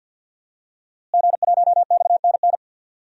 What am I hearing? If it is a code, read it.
Morse code, heard as a keyed tone, about 47 wpm, 700 Hz. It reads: G1XNN
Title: DXCC-100 Call Signs - 5 Characters - Repeated 2x - 50wpm